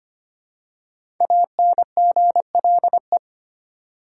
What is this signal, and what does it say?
Morse code, heard as a keyed tone, about 25 wpm, 700 Hz. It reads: ANGLE
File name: 5 Letter Words - First 3 Letters Given - 25wpm